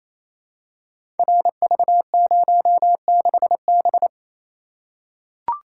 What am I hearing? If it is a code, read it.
Morse code, heard as a keyed tone, about 28 wpm, 700 Hz. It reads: RV06B
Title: Code Groups - 5 Letter:Numbers - 28wpm